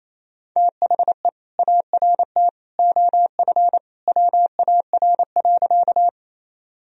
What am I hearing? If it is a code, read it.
Morse code, heard as a keyed tone, about 27 wpm, 700 Hz. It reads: THE ART OF WAR.
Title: Sentences from Top 1000 Words - Review Part 3 - 28wpm